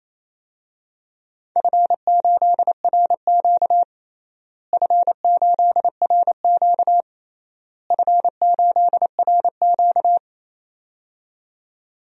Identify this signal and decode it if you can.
Morse code, heard as a keyed tone, about 28 wpm, 700 Hz. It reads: F8RQ F8RQ F8RQ
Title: DXCC-100 Call Signs - 4 Characters - Repeated 3x - 28wpm